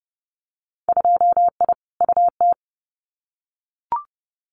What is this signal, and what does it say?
Morse code, heard as a keyed tone, about 30 wpm, 700 Hz. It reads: 2I UT
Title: Winter Field Day Exchanges 30wpm